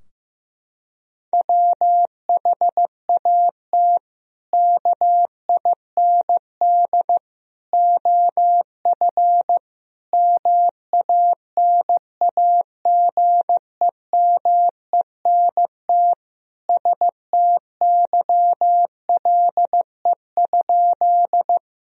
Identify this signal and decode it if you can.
Morse code, heard as a keyed tone, about 15 wpm, 700 Hz. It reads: WHAT KIND OF MANAGEMENT STYLE?
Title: Sentences from Top 1000 Words - Review Part 4 - 15wpm